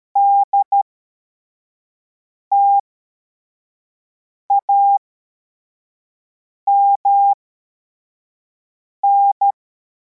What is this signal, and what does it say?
Morse code, heard as a keyed tone, about 13 wpm, spaced as for 4 wpm, 800 Hz. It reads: DTAMN